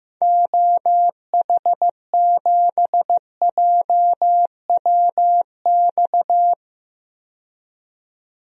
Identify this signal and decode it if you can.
Morse code, heard as a keyed tone, about 15 wpm, 700 Hz. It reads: OH7JWX
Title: DXCC-100 Call Signs - 6 Characters - 15wpm